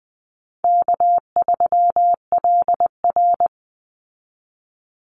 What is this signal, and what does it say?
Morse code, heard as a keyed tone, about 20 wpm, 700 Hz. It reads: K3LR
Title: Easy Call Signs Repeated 3x - 20wpm